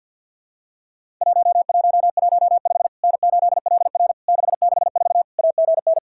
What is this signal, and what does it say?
Morse code, heard as a keyed tone, about 50 wpm, 700 Hz. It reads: 1115 N8LR 664 AGN
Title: Worked All Europe DX Contest - QTC Exchange with Repeats - 50wpm